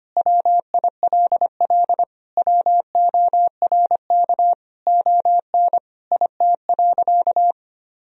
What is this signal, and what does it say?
Morse code, heard as a keyed tone, about 25 wpm, 700 Hz. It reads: WILL WORK ON IT.
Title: Sentences from Top 1000 Words - Review Part 3 - 25wpm